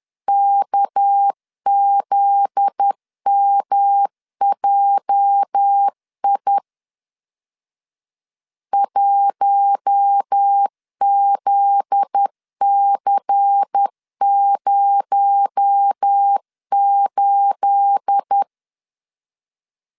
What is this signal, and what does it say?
Morse code, heard as a keyed tone, about 10 wpm, 800 Hz. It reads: KZMJI 1ZC08
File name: CW / DET / 1 9 1500C S B 10wpm 4439